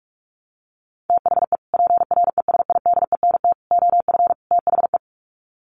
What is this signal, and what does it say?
Morse code, heard as a keyed tone, about 44 wpm, 700 Hz. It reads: THE PRESIDENT OF THE
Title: Sets of 4 Words 45wpm